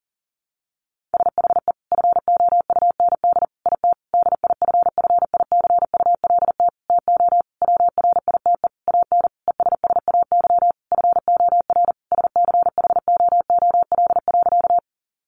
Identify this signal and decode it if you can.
Morse code, heard as a keyed tone, about 40 wpm, 700 Hz. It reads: SHE FOUND IT DIFFICULT TO WRITE AN ESSAY FOR SCHOOL.
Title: Sentences from Top 1500 Words 40wpm